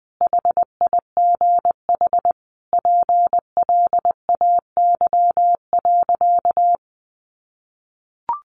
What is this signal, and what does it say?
Morse code, heard as a keyed tone, about 20 wpm, 700 Hz. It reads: HIGH PLAY.
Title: Sentences from Top 1500 Words 20wpm